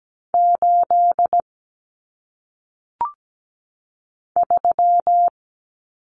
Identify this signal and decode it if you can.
Morse code, heard as a keyed tone, about 17 wpm, 700 Hz. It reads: 8 3
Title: Numbers - 1-Digit 17wpm